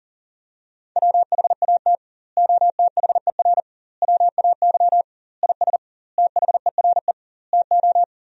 Morse code, heard as a keyed tone, about 37 wpm, 700 Hz. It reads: WHAT OTHER WAY IS THERE TO
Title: 2x Word-Spacing - Sentences from Top 100 Words - 40wpm